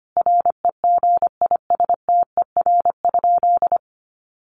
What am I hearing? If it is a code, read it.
Morse code, heard as a keyed tone, about 25 wpm, 700 Hz. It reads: REGISTER?
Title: Sentences from Top 1200 Words 25wpm